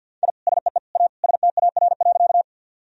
Morse code, heard as a keyed tone, about 53 wpm, 700 Hz. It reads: I SEE A STAR.
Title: Sentences from Top 1000 Words - Review Part 2 - 50wpm